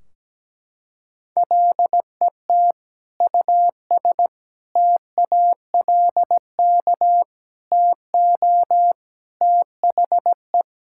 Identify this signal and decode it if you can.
Morse code, heard as a keyed tone, about 17 wpm, 700 Hz. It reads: LET US TALK TO THE
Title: Sentences from Top 1600 Words 17wpm